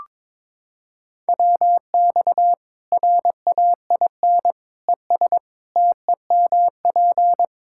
Morse code, heard as a keyed tone, about 22 wpm, 700 Hz. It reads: WX RAIN ES TEMP
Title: Practice QSO - Exchange 2 - Info - 22wpm